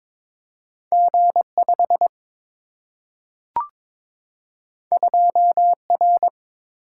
Morse code, heard as a keyed tone, about 22 wpm, 700 Hz. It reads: G5 2R
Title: Single-Letter Single-Number 22wpm